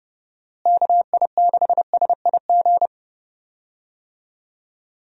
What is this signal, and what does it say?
Morse code, heard as a keyed tone, about 30 wpm, 700 Hz. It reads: KI6SIG
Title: USA Call Signs - 6 Characters - Repeated 3x - 30wpm